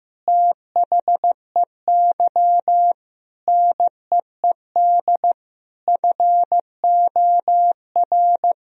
Morse code, heard as a keyed tone, about 15 wpm, 700 Hz. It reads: THEY NEED FOR
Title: Sentences from Top 1000 Words - Review Part 4 - 15wpm